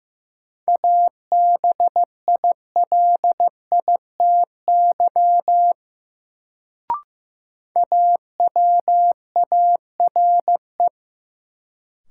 Morse code, heard as a keyed tone, about 15 wpm, 700 Hz. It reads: ABILITY AWARE
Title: Sentences from Top 1200 Words - New Word Drill Alt - 15wpm